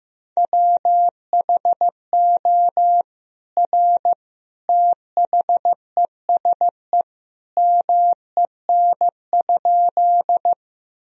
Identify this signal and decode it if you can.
Morse code, heard as a keyed tone, about 15 wpm, 700 Hz. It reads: WHO R THESE MEN?
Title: Sentences from Top 200 Words and Abbreviations 15wpm